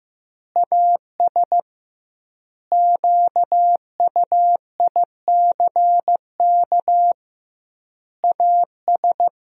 Morse code, heard as a keyed tone, about 15 wpm, 700 Hz. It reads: AS QUICK AS
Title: 2x Word-Spacing - Sentences from Top 500 Words - 15wpm